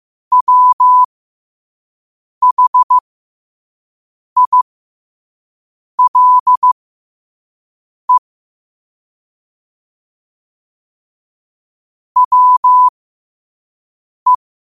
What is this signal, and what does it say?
Morse code, heard as a keyed tone, about 15 wpm, spaced as for 5 wpm, 1000 Hz. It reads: WHILE WE
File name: qotdfiles / 5wpm